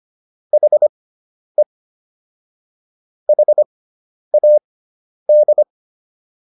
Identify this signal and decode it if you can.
Morse code, heard as a keyed tone, about 25 wpm, spaced as for 10 wpm, 600 Hz. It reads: HE HAD